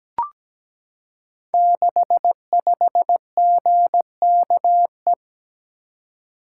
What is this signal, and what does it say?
Morse code, heard as a keyed tone, about 17 wpm, 690 Hz. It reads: 65GKE
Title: Code Groups - 5 Letter:Numbers - 17wpm